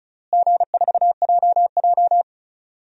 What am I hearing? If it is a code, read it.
Morse code, heard as a keyed tone, about 36 wpm, 700 Hz. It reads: G4JJ
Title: DXCC-100 Call Signs - 4 Characters - Repeated 3x - 35wpm